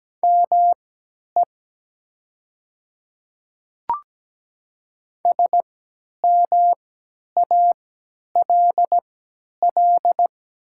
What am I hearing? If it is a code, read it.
Morse code, heard as a keyed tone, about 17 wpm, spaced as for 10 wpm, 700 Hz. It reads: ME SMALL
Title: 3x Character Spacing - Top 300 Words - 17wpm